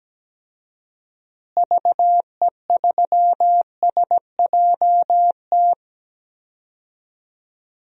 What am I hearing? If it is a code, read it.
Morse code, heard as a keyed tone, about 17 wpm, 700 Hz. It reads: VE3SJT
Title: DXCC-100 Call Signs - 6 Characters - Repeated 3x - 17wpm